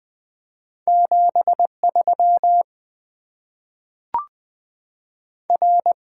Morse code, heard as a keyed tone, about 20 wpm, 700 Hz. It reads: 73 R
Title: Top 20 QSO Elements - 8x Character Spacing - 20wpm